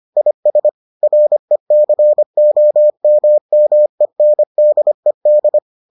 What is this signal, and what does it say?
Morse code, heard as a keyed tone, about 25 wpm, 600 Hz. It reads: IS RECOMMENDED